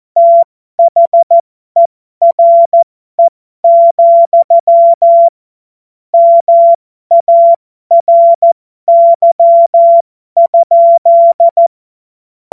Morse code, heard as a keyed tone, about 14 wpm, spaced as for 12 wpm, 680 Hz. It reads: THERE, MARY?
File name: Morsekurs / Witze